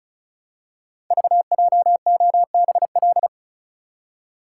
Morse code, heard as a keyed tone, about 36 wpm, 700 Hz. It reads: VJOBL
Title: Code Groups - 5 Letters - 35wpm